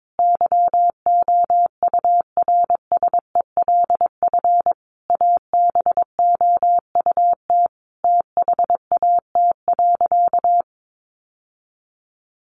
Morse code, heard as a keyed tone, about 22 wpm, 700 Hz. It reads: YOURSELF ABOUT THAT.
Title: Sentences from Top 1000 Words 22wpm